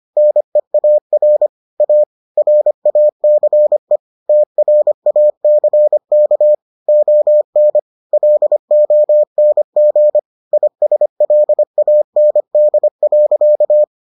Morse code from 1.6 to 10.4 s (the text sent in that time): A RACE TRACK ON LONG